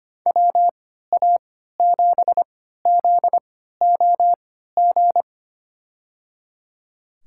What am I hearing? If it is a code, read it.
Morse code, heard as a keyed tone, about 25 wpm, spaced as for 14 wpm, 700 Hz. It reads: WA7ZOG
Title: USA Call Signs - 6 Characters - 3x Character Spacing - 25wpm